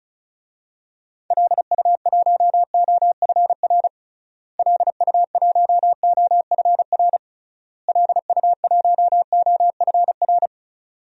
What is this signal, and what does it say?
Morse code, heard as a keyed tone, about 36 wpm, 700 Hz. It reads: LU1OFR LU1OFR LU1OFR
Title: DXCC-100 Call Signs - 6 Characters - Repeated 3x - 35wpm